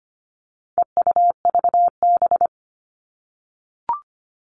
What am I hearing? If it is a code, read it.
Morse code, heard as a keyed tone, about 25 wpm, 700 Hz. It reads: EUVB
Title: All Cut Numbers - 4-Digit 25wpm